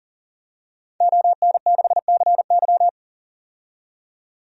Morse code, heard as a keyed tone, about 40 wpm, 700 Hz. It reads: ON6CY
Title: DXCC-100 Call Signs - 5 Characters - 40wpm